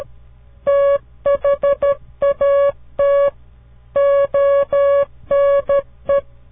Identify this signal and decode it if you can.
Morse code, heard as a keyed tone, about 13 wpm, 560 Hz. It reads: THAT ONE
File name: CW Learning Programs / farnsworth / unit 2